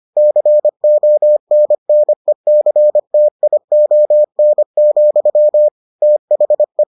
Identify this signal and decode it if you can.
Morse code, heard as a keyed tone, about 25 wpm, 600 Hz. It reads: CONNECTION, THE